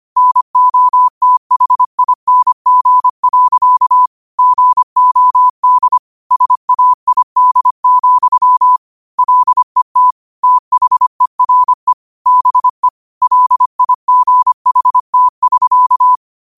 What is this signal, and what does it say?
Morse code, heard as a keyed tone, about 25 wpm, 1000 Hz. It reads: NOTHING. GOD SAID, LET THERE BE LIGHT<SK>
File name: qotdfiles / 25wpm